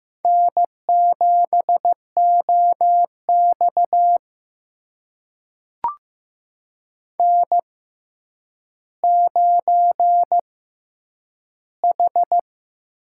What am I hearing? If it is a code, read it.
Morse code, heard as a keyed tone, about 15 wpm, spaced as for 9 wpm, 700 Hz. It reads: N7OX N9H
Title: USA Call Signs - 6x Character Spacing - 15wpm